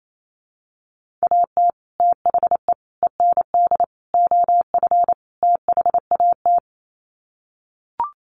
Morse code, heard as a keyed tone, about 28 wpm, 700 Hz. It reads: AT THE END OF THAT
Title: Sets of 5 Words Encore2 28wpm